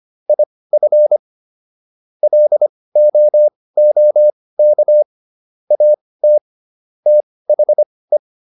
Morse code, heard as a keyed tone, about 25 wpm, spaced as for 18 wpm, 600 Hz. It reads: IF LOOK AT THE